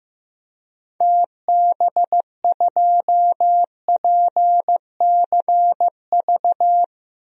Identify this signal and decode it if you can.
Morse code, heard as a keyed tone, about 15 wpm, 700 Hz. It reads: TB2PCV
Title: DXCC-100 Call Signs - 6 Characters - 15wpm